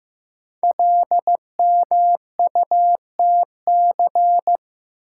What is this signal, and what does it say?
Morse code, heard as a keyed tone, about 15 wpm, 700 Hz. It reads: LMUTC
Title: Code Groups - 5 Letters - 15wpm